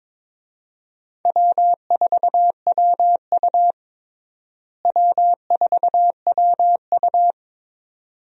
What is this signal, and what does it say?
Morse code, heard as a keyed tone, about 22 wpm, 700 Hz. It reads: W4WU W4WU
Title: USA Call Signs - 4 Characters - Repeated 2x - 22wpm